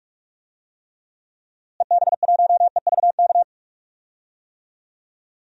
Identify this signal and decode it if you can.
Morse code, heard as a keyed tone, about 46 wpm, 690 Hz. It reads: EB1EVK
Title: DXCC-100 Call Signs - 6 Characters - Repeated 2x - 45wpm